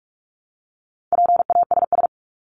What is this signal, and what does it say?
Morse code, heard as a keyed tone, about 44 wpm, 700 Hz. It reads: PASS
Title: Top 500 Words Repeated 45wpm